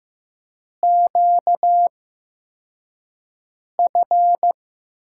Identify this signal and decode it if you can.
Morse code, heard as a keyed tone, about 15 wpm, spaced as for 4 wpm, 700 Hz. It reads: QF
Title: USA Call Signs - 4 Characters - 8x Character Spacing - 15wpm